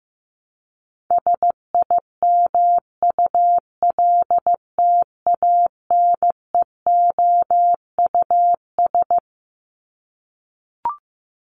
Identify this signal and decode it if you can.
Morse code, heard as a keyed tone, about 15 wpm, 700 Hz. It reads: SIMULTANEOUS
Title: LY Suffix - Top 100 Words with Base Words 15wpm